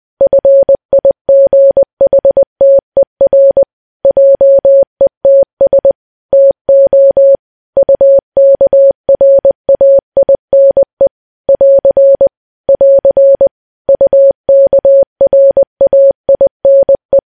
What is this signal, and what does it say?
Morse code, heard as a keyed tone, about 20 wpm, 550 Hz. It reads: FIGHTER JETS TO UKRAINE <AR> <AR> UKRAINE
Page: news-headlines-in-morse-code-at-20-wpm